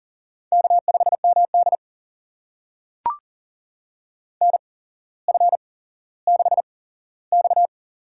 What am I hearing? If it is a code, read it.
Morse code, heard as a keyed tone, about 40 wpm, spaced as for 17 wpm, 700 Hz. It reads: K5MD NF6X
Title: USA Call Signs - 4 Characters - 8x Character Spacing - 40wpm